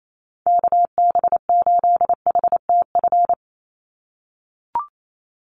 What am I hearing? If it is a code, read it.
Morse code, heard as a keyed tone, about 28 wpm, 700 Hz. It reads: KB8HTF
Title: USA Call Signs 28wpm